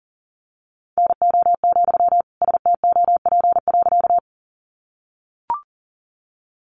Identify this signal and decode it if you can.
Morse code, heard as a keyed tone, about 40 wpm, 700 Hz. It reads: NO, STOP.
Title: Sentences from Top 300 Words 40wpm